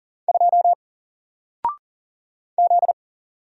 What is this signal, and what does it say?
Morse code, heard as a keyed tone, about 40 wpm, 700 Hz. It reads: 2 Z
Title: Single Letter-Number Alt 40wpm